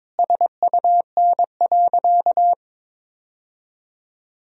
Morse code, heard as a keyed tone, about 22 wpm, 700 Hz. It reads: SUN.
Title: Sentences from Top 1000 Words - Review Part 1 - 22wpm